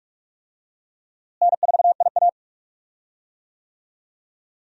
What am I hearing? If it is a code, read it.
Morse code, heard as a keyed tone, about 46 wpm, 700 Hz. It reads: N4IA
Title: USA Call Signs - 4 Characters - Repeated 2x - 45wpm